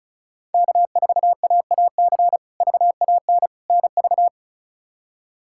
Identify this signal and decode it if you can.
Morse code, heard as a keyed tone, about 34 wpm, 700 Hz. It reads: K4AAC VAN NV
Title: K1USN SST QSO Practice - 35wpm